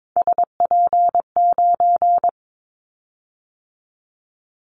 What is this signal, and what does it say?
Morse code, heard as a keyed tone, about 22 wpm, 700 Hz. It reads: SP9
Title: Callsign Prefixes for Top 25 DXCC - 22wpm